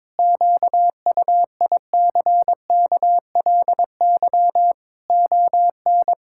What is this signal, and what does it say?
Morse code, heard as a keyed tone, about 21 wpm, 700 Hz. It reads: QUICKLY ON
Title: Pangram Challenge v2 - 22wpm